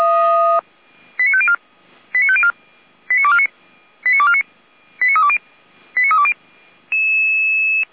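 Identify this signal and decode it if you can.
Five-tone selective call (ZVEI), heard as quick runs of five tones, tones of about 70 ms. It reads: 89504 89504 89249 89249 89320 89320
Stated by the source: ZVEI five-tone selective call received with an SDR, recorded by Feuerwehr Orlamünde fire brigade.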